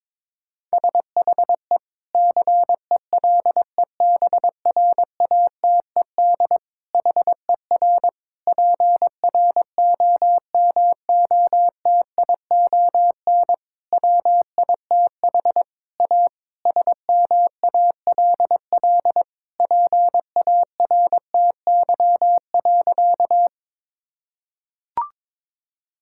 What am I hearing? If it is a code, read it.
Morse code, heard as a keyed tone, about 22 wpm, 700 Hz. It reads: SHE CELEBRATED HER PROMOTION WITH A SMALL PARTY.
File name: Difficult Sentences with Hints - 22wpm